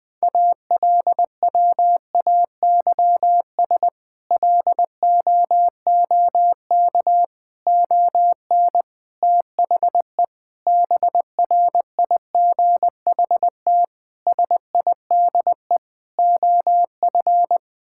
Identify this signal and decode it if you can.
Morse code, heard as a keyed tone, about 20 wpm, 700 Hz. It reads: ALWAYS LOOK ON THE BRIGHT SIDE OF
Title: Sets of 8 Words - 20wpm